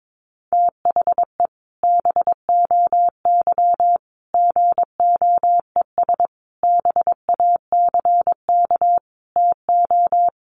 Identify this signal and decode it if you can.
Morse code, heard as a keyed tone, about 22 wpm, 700 Hz. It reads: THE BOY GOES BACK TO
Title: Easy Phrases 22wpm